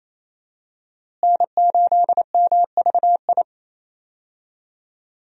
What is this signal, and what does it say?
Morse code, heard as a keyed tone, about 28 wpm, 700 Hz. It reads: N8MVI
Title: USA Call Signs - 5 Characters - Repeated 2x - 28wpm